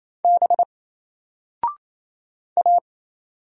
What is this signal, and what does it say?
Morse code, heard as a keyed tone, about 27 wpm, 700 Hz. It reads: B A
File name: Single-Letter-Number-28wpm